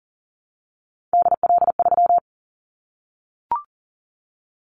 Morse code, heard as a keyed tone, about 43 wpm, 700 Hz. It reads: DL3
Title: Callsign Prefixes for Top 50 DXCC - 40wpm